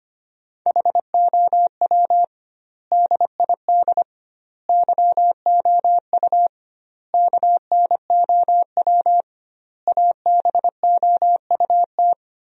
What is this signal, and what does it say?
Morse code, heard as a keyed tone, about 25 wpm, 700 Hz. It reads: HOW DID YOU KNOW ABOUT
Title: 2x Word-Spacing - Sentences from Top 100 Words - 25wpm